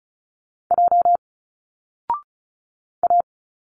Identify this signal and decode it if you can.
Morse code, heard as a keyed tone, about 36 wpm, 700 Hz. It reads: J A
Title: Single Letters 35wpm